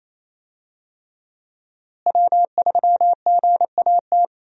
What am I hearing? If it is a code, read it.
Morse code, heard as a keyed tone, about 28 wpm, 700 Hz. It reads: W3GAT
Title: USA Call Signs - 5 Characters - Repeated 3x - 28wpm